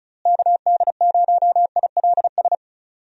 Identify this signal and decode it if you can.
Morse code, heard as a keyed tone, about 36 wpm, 700 Hz. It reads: KD0ILS
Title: USA Call Signs - 6 Characters - Repeated 3x - 35wpm